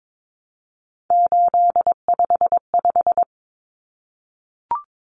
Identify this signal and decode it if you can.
Morse code, heard as a keyed tone, about 22 wpm, 700 Hz. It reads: 855
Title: Common Cut Numbers - 3-Digit 22wpm